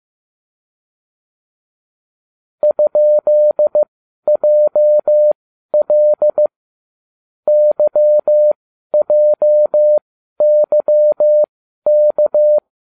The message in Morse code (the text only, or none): ?JL YJYK